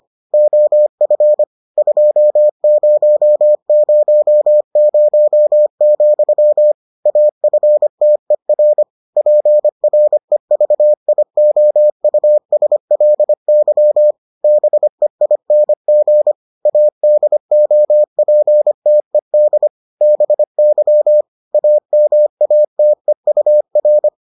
Morse code, heard as a keyed tone, about 25 wpm, 600 Hz. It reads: OF 2000, AFTER PREVIOUSLY BEING ADOPTED BY AMATEUR